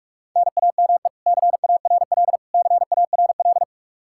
Morse code, heard as a keyed tone, about 46 wpm, 700 Hz. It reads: NAME CARL CARL
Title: K1USN SST QSO Practice 45wpm